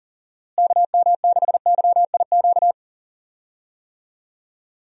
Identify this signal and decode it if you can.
Morse code, heard as a keyed tone, about 40 wpm, 700 Hz. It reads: KM6YIQ
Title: USA Call Signs - 6 Characters - Repeated 2x - 40wpm